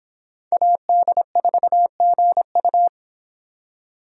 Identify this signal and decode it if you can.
Morse code, heard as a keyed tone, about 26 wpm, 700 Hz. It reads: AD4GU